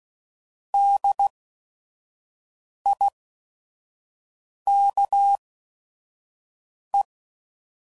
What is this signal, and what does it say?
Morse code, heard as a keyed tone, about 16 wpm, spaced as for 5 wpm, 800 Hz. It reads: DIKE
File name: Prefixes / By First Letter / Ds